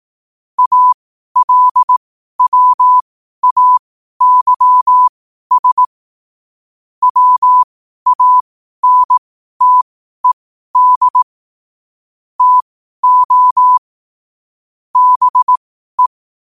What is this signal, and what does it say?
Morse code, heard as a keyed tone, about 18 wpm, spaced as for 12 wpm, 1000 Hz. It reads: ALWAYS WANTED TO BE
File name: qotdfiles / 12wpm